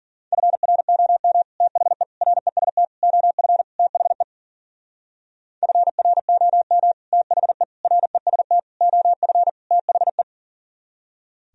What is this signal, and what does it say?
Morse code, heard as a keyed tone, about 53 wpm, 700 Hz. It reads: FROM THE REST OF THE FROM THE REST OF THE